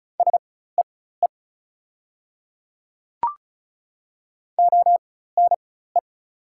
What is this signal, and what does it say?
Morse code, heard as a keyed tone, about 36 wpm, spaced as for 16 wpm, 700 Hz. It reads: SEE ONE
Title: 4x Character Spacing - Top 100 Words - 35wpm